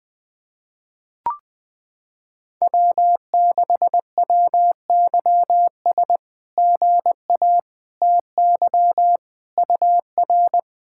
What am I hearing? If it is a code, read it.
Morse code, heard as a keyed tone, about 20 wpm, 700 Hz. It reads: W6WYS GA TY UR
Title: SOTA QSO Practice 20wpm